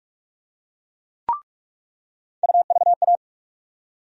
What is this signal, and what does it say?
Morse code, heard as a keyed tone, about 46 wpm, 700 Hz. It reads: UVA
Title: USA Call Sign Postfixes - 3 Letters 45wpm